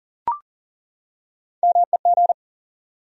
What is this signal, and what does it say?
Morse code, heard as a keyed tone, about 42 wpm, 700 Hz. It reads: MEG
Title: USA Common Names - Max 3 Letters - 40wpm